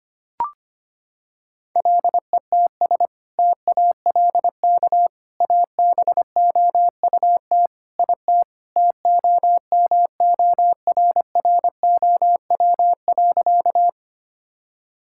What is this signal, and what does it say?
Morse code, heard as a keyed tone, about 25 wpm, 700 Hz. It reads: LETS TALK ABOUT IT TOMORROW.
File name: practice / 25wpm / Sentences with 2-Word Phrases - 25wpm